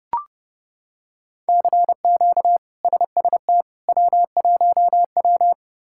Morse code, heard as a keyed tone, about 31 wpm, 700 Hz. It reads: CQ SST W1W
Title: K1USN SST QSO Practice - 30wpm